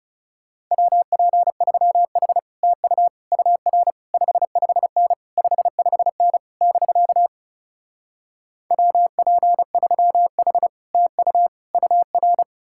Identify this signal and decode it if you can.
Morse code, heard as a keyed tone, about 40 wpm, 700 Hz. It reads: WP3H TU UR 55N 55N <BK> WP3H TU UR